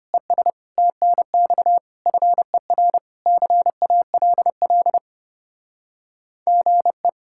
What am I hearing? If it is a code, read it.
Morse code, heard as a keyed tone, about 31 wpm, 700 Hz. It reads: ES TNX FER CALL GE
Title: Speed-Racing Common QSO Phrases Encore - 25wpm